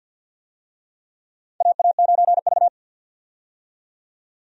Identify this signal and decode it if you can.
Morse code, heard as a keyed tone, about 53 wpm, 700 Hz. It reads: AA9V
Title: USA Call Signs - 4 Characters - 50wpm